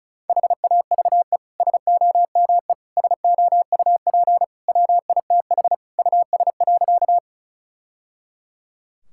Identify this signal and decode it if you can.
Morse code, heard as a keyed tone, about 36 wpm, 700 Hz. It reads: HAVE SOME SOUP WITH US.
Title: Sentences from Top 1000 Words - Review Part 6 - 35wpm